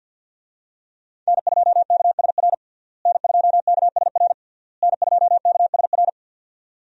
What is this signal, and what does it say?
Morse code, heard as a keyed tone, about 52 wpm, 700 Hz. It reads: N2KSR N2KSR N2KSR
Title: USA Call Signs - 5 Characters - Repeated 3x - 50wpm